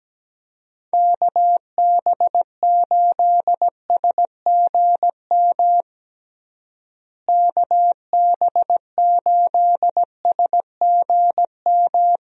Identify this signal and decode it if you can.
Morse code, heard as a keyed tone, about 17 wpm, 700 Hz. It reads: KB8SGM KB8SGM
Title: USA Call Signs - 6 Characters - Repeated 3x - 17wpm